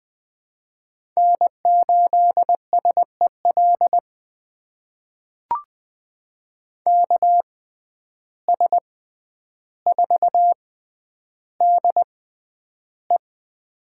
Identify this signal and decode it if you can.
Morse code, heard as a keyed tone, about 20 wpm, spaced as for 10 wpm, 700 Hz. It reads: N8SEL KS4DE
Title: USA Call Signs - 5 Characters - 6x Character Spacing - 20wpm